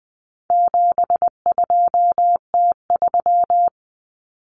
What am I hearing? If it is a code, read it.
Morse code, heard as a keyed tone, about 20 wpm, 700 Hz. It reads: 72T3
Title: Common Cut Numbers - 4-Digit 20wpm